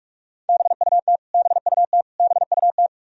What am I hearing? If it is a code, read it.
Morse code, heard as a keyed tone, about 46 wpm, 700 Hz. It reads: BUT BUT BUT
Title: Top 100 Words Repeated 45wpm